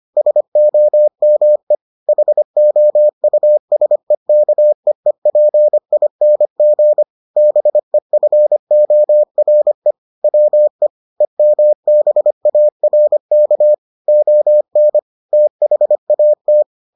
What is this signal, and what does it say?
Morse code, heard as a keyed tone, about 25 wpm, 600 Hz. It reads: SOME HOUSEKEEPING BEFORE WE EMBARK ON THAT